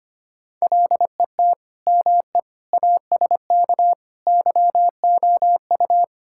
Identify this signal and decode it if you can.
Morse code, heard as a keyed tone, about 25 wpm, 700 Hz. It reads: LET ME ASK YOU
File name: practice / 25wpm / Sentences with 4-Word Phrases - 25wpm